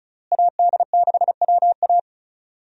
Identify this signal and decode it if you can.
Morse code, heard as a keyed tone, about 36 wpm, 700 Hz. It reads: AD6WA
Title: USA Call Signs - 5 Characters - Repeated 3x - 35wpm